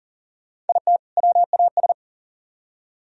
Morse code, heard as a keyed tone, about 50 wpm, 690 Hz. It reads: IT WAS
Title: Speed-Racing - Sets of 2 Words 40wpm